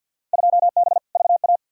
Morse code, heard as a keyed tone, about 54 wpm, 700 Hz. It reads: 2B VA
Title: Field Day Exchanges Encore - 50wpm